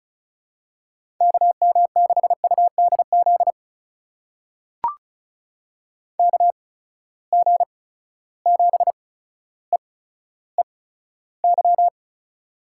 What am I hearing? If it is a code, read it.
Morse code, heard as a keyed tone, about 36 wpm, spaced as for 15 wpm, 700 Hz. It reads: KM6UDZ KG7EEY
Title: USA Call Signs - 6 Characters - 8x Character Spacing - 35wpm